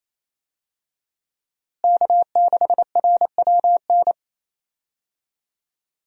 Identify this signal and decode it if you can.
Morse code, heard as a keyed tone, about 28 wpm, 700 Hz. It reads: K6RWN
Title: USA Call Signs - 5 Characters - Repeated 2x - 28wpm